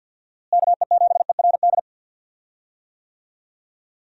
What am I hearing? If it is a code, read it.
Morse code, heard as a keyed tone, about 49 wpm, 700 Hz. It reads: KE7ERD
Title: USA Call Signs - 6 Characters - Repeated 2x - 50wpm